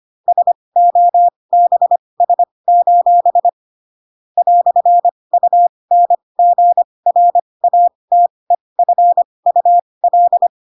Morse code, heard as a keyed tone, about 25 wpm, spaced as for 20 wpm, 700 Hz. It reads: SOBS: "UNGRATEFUL